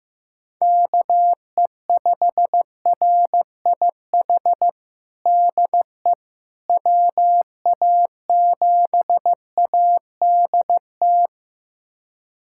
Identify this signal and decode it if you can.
Morse code, heard as a keyed tone, about 15 wpm, 700 Hz. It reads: KE5RIH DE WA7ADT
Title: USA Callsign DE Callsign - 6 Characters - 15wpm